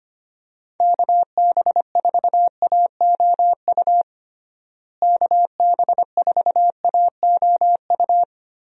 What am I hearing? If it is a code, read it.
Morse code, heard as a keyed tone, about 25 wpm, 700 Hz. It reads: KB4AOU KB4AOU
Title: USA Call Signs - 6 Characters - Repeated 3x - 25wpm